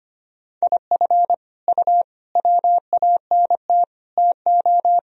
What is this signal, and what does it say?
Morse code, heard as a keyed tone, about 25 wpm, 700 Hz. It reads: IF U WANT TO
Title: Sentences from Top 200 Words and Abbreviations 25wpm